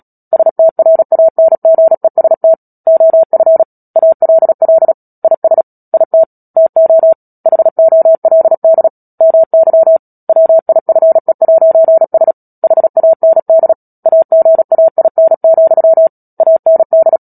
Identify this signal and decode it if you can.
Morse code, heard as a keyed tone, about 37 wpm, 650 Hz. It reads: STRANGEST OF ALL IS IT TO HOLD MY WIFE'S HAND AGAIN, AND
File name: Book Two / 10-Chapter 10 - Epilogue - 36 WPM